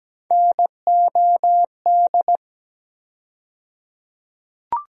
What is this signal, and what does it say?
Morse code, heard as a keyed tone, about 17 wpm, 700 Hz. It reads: NOD
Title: USA Call Sign Postfixes - 3 Letters 17wpm